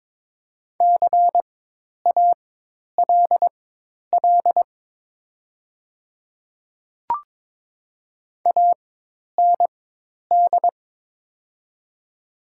Morse code, heard as a keyed tone, about 22 wpm, spaced as for 10 wpm, 700 Hz. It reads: CALL AND
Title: 4x Character Spacing - Top 100 Words - 22wpm